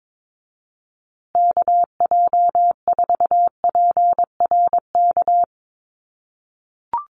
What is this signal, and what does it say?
Morse code, heard as a keyed tone, about 22 wpm, 700 Hz. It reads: KJ4PRK
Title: USA Call Signs Repeated 3x 22wpm